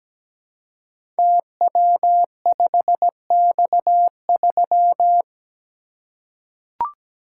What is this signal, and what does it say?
Morse code, heard as a keyed tone, about 17 wpm, 700 Hz. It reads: TW5X3
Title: Code Groups - 5 Letter:Numbers - 17wpm